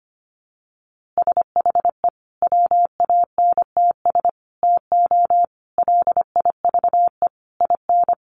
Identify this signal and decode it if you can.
Morse code, heard as a keyed tone, about 25 wpm, 700 Hz. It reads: SHE WANTS TO LIVE IN
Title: Easy Phrases 25wpm